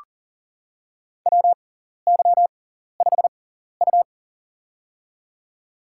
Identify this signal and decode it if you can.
Morse code, heard as a keyed tone, about 37 wpm, spaced as for 14 wpm, 700 Hz. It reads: WY5U
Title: USA Call Signs - 4 Characters - 6x Character Spacing - 40wpm